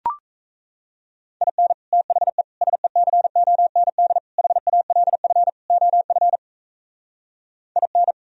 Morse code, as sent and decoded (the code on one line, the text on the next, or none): .. -. / - .... / ... -.-. --- -. -.. / .... .- .-.. ..-. / --- ..-. / .. -.
IN THE SECOND HALF OF IN